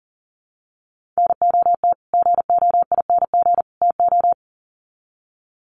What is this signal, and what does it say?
Morse code, heard as a keyed tone, about 40 wpm, 700 Hz. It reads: NOT GOING TO
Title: Sets of 3 Words 40wpm